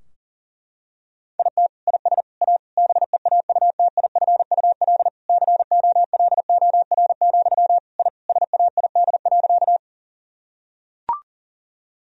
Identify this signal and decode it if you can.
Morse code, heard as a keyed tone, about 40 wpm, 700 Hz. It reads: IT IS A BEAUTIFUL COLOR, I SAID.
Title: Sentences from Top 1000 Words - Review Part 2 - 40wpm